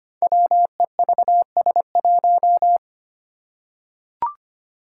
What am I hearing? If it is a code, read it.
Morse code, heard as a keyed tone, about 25 wpm, 700 Hz. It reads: WEVS1
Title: Five Challenging Letter-Number Combinations 25wpm